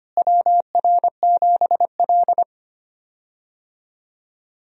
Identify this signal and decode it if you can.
Morse code, heard as a keyed tone, about 25 wpm, 700 Hz. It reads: WR7L
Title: USA Call Signs - 4 Characters - Repeated 3x - 25wpm